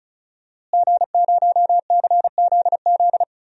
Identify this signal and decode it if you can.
Morse code, heard as a keyed tone, about 34 wpm, 700 Hz. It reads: G0CZZ
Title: DXCC-100 Call Signs - 5 Characters - Repeated 3x - 35wpm